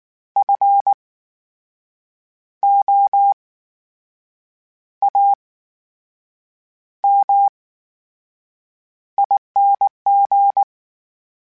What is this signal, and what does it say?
Morse code, heard as a keyed tone, about 19 wpm, spaced as for 6 wpm, 800 Hz. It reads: FOAMING
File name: Special INGs / Special INGs-mp3s / VerySlowToFaster Ing @18